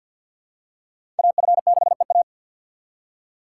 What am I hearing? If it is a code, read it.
Morse code, heard as a keyed tone, about 52 wpm, 700 Hz. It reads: AF6EA
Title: USA Call Signs - 5 Characters - 2x Character Spacing - 50wpm